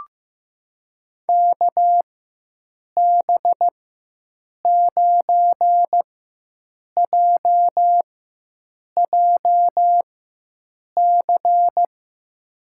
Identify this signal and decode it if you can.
Morse code, heard as a keyed tone, about 15 wpm, spaced as for 7 wpm, 700 Hz. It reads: KB9JJC
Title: USA Call Signs - 4x Character Spacing - 15wpm